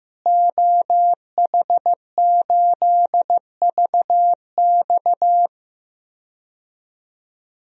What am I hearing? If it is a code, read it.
Morse code, heard as a keyed tone, about 15 wpm, 700 Hz. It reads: OH8VX
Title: DXCC-100 Call Signs - 5 Characters - Repeated 2x - 15wpm